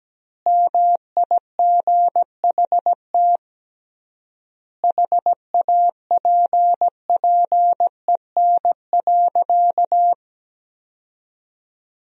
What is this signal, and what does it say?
Morse code, heard as a keyed tone, about 17 wpm, 700 Hz. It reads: MIGHT HAPPEN.
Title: 3x Word-Spacing - Sentences from Top 400 Words - 17wpm